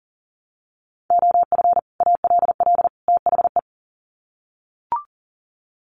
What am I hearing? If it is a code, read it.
Morse code, heard as a keyed tone, about 37 wpm, 700 Hz. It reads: OF ALL THE
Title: Sets of 3 Words 40wpm